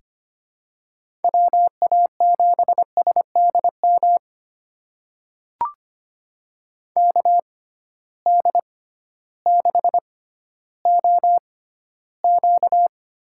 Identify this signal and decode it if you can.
Morse code, heard as a keyed tone, about 25 wpm, spaced as for 14 wpm, 700 Hz. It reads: WA7SDM KD6OQ
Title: USA Call Signs - 6 Characters - 6x Character Spacing - 25wpm